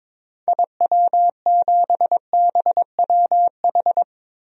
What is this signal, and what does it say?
Morse code, heard as a keyed tone, about 21 wpm, 700 Hz. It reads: IW7BWH
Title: DXCC-100 Call Signs - 6 Characters - Repeated 2x - 22wpm